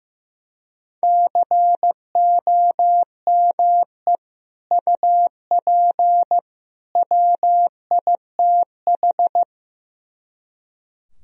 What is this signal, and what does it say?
Morse code, heard as a keyed tone, about 15 wpm, 700 Hz. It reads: COME UP WITH
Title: Speed-Racing - Sets of 3 Words 15wpm